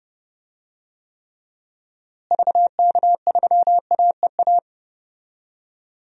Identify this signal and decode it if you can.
Morse code, heard as a keyed tone, about 32 wpm, 700 Hz. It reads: VK3AEA
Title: DXCC-100 Call Signs - 6 Characters - Repeated 2x - 30wpm